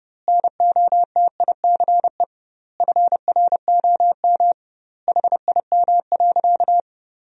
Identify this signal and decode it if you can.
Morse code, heard as a keyed tone, about 30 wpm, 700 Hz. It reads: NOTICE FROM HIM.
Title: 2x Word-Spacing - Sentences from Top 400 Words - 30wpm